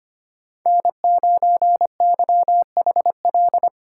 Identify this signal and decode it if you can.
Morse code, heard as a keyed tone, about 31 wpm, 700 Hz. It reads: N9YHL
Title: Speed Racing - USA Call Signs 25wpm